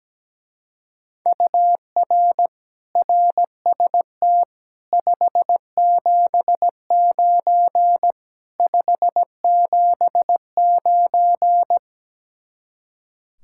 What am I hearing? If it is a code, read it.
Morse code, heard as a keyed tone, about 17 wpm, 700 Hz. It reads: UR RST 579 579
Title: Speed-Racing Common QSO Phrases Encore - 17wpm